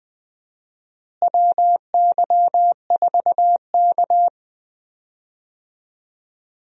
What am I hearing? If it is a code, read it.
Morse code, heard as a keyed tone, about 20 wpm, 700 Hz. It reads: WY4K
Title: USA Call Signs - 4 Characters - Repeated 2x - 20wpm